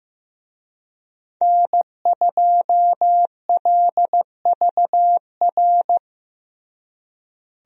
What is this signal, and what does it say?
Morse code, heard as a keyed tone, about 16 wpm, 700 Hz. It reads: N2LVR
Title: USA Call Signs - 5 Characters - Repeated 3x - 15wpm